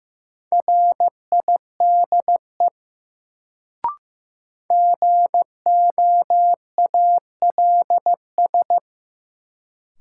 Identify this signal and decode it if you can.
Morse code, heard as a keyed tone, about 15 wpm, 700 Hz. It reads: RIDE GOALS
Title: Sentences from Top 1000 Words - Review Part 6 - New Word Drill - 15wpm